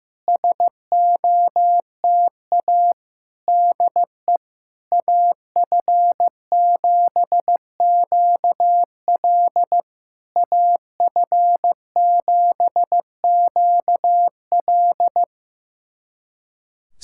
Morse code, heard as a keyed tone, about 15 wpm, 700 Hz. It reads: SOTA DE AF7QL AF7QL
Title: SOTA QSO Practice 15wpm